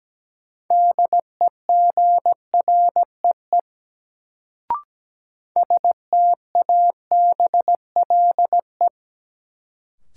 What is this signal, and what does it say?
Morse code, heard as a keyed tone, about 17 wpm, 700 Hz. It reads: DEGREE STABLE
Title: Sentences from Top 1500 Words - New Word Drill Alt - 17wpm